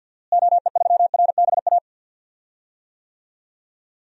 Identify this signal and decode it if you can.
Morse code, heard as a keyed tone, about 52 wpm, 700 Hz. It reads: OE3RBA
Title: DXCC-100 Call Signs - 6 Characters - Repeated 3x - 50wpm